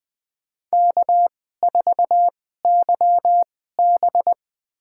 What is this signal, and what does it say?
Morse code, heard as a keyed tone, about 20 wpm, spaced as for 15 wpm, 700 Hz. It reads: K4YB
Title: USA Call Signs - 4 Characters - 2x Character Spacing - 20wpm